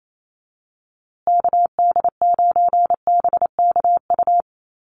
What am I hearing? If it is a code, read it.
Morse code, heard as a keyed tone, about 28 wpm, 700 Hz. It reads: KD9BKU
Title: USA Call Signs Repeated 2x 28wpm